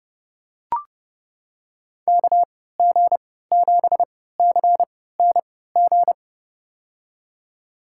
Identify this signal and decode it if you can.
Morse code, heard as a keyed tone, about 31 wpm, spaced as for 17 wpm, 700 Hz. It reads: KG7CNG
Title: USA Call Signs - 6 Characters - 3x Character Spacing - 30wpm